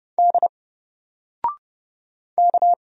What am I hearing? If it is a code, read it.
Morse code, heard as a keyed tone, about 30 wpm, 700 Hz. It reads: D K
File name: Single-Letter-Number-30wpm